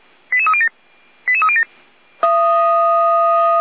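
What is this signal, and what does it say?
Five-tone selective call (ZVEI), heard as quick runs of five tones, tones of about 70 ms. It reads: 88307 88307
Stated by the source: Five-tone ZVEI selective call received by SDR.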